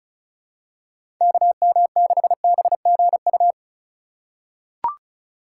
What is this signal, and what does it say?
Morse code, heard as a keyed tone, about 36 wpm, 700 Hz. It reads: KM6BGU
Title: USA Call Signs - 6 Characters - 2x Character Spacing - 35wpm